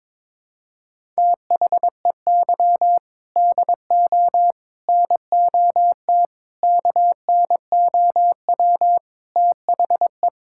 Morse code, heard as a keyed tone, about 22 wpm, 700 Hz. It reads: THEY DO NOT KNOW THE
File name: practice / 22wpm / Sentences with 4-Word Phrases - 22wpm